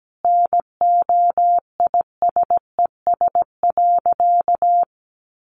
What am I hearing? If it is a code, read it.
Morse code, heard as a keyed tone, about 17 wpm, 700 Hz. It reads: NOISES.
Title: Easy Phrases 17wpm